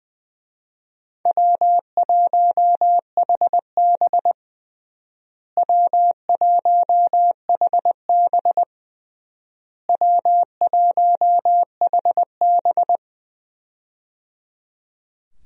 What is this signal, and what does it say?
Morse code, heard as a keyed tone, about 20 wpm, 700 Hz. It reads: W1HB W1HB W1HB
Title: USA Call Signs - 4 Characters - Repeated 3x - 20wpm